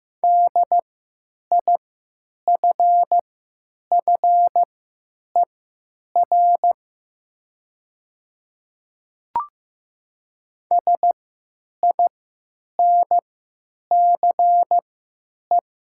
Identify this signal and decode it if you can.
Morse code, heard as a keyed tone, about 15 wpm, spaced as for 9 wpm, 700 Hz. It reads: DIFFER SINCE
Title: 3x Character Spacing - Top 500 Words - 15wpm